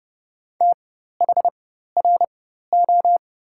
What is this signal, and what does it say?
Morse code, heard as a keyed tone, about 32 wpm, spaced as for 14 wpm, 700 Hz. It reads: THRO
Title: 4x Character Spacing - Top 300 Words - 30wpm